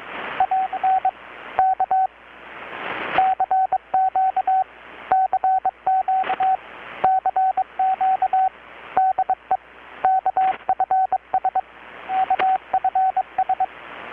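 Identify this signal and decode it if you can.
Morse code, heard as a keyed tone, about 23 wpm, 760 Hz, with radio noise. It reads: AN K CQ CQ CQ DE KFS KFS
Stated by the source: Morse from KPH's Night of Nights XX maritime radiotelegraphy event, received in Chiapas, Mexico about 3,800 km away on a Kenwood R-600 with a grounded 10 m long-wire antenna.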